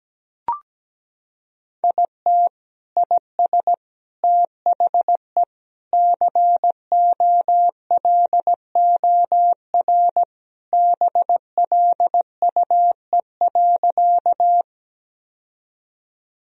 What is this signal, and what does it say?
Morse code, heard as a keyed tone, about 17 wpm, 700 Hz. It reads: IT IS THE COLOR BLUE.
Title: Sentences from Top 1000 Words - Review Part 3 - 17wpm